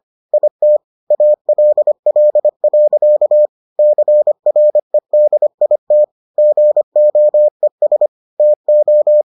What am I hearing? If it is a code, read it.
Morse code, heard as a keyed tone, about 25 wpm, 600 Hz. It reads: IT ALL. CREDIT GOES TO